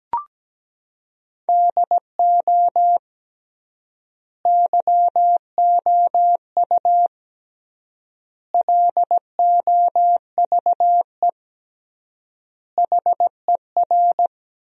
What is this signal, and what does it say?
Morse code, heard as a keyed tone, about 17 wpm, 700 Hz. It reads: DO YOU LOVE HER
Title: 3x Word-Spacing - Sentences from Top 400 Words - 17wpm